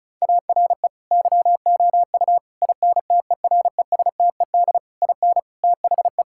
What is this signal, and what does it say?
Morse code, heard as a keyed tone, about 34 wpm, 700 Hz. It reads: ARE YOU INTERESTED IN THE
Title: Sentences from Top 1000 Words - Review Part 6 - 35wpm